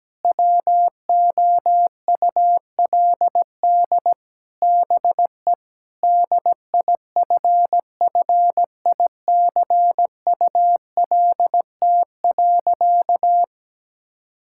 Morse code, heard as a keyed tone, about 17 wpm, 700 Hz. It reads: WOULD BE DIFFICULT.
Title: Sentences from Top 1600 Words 17wpm